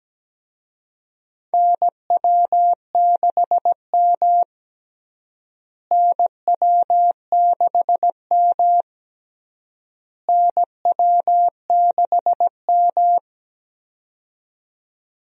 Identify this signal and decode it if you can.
Morse code, heard as a keyed tone, about 17 wpm, 700 Hz. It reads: NW6M NW6M NW6M
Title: USA Call Signs - 4 Characters - Repeated 3x - 17wpm